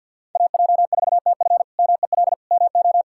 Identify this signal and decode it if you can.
Morse code, heard as a keyed tone, about 51 wpm, 700 Hz. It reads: AJ4TF MEL MO
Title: K1USN SST QSO Practice - 50wpm